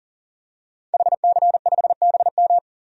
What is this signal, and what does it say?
Morse code, heard as a keyed tone, about 37 wpm, 700 Hz. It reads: HC5BM
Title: DXCC-100 Call Signs - 5 Characters - 40wpm